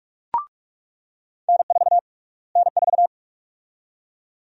Morse code, heard as a keyed tone, about 42 wpm, 700 Hz. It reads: N4 N4
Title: USA Call Sign Prefixes - Repeated 2x - 45wpm